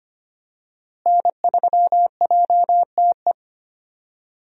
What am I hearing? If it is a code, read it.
Morse code, heard as a keyed tone, about 25 wpm, 700 Hz. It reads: N3JTE
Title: USA Call Signs - 5 Characters - Repeated 3x - 25wpm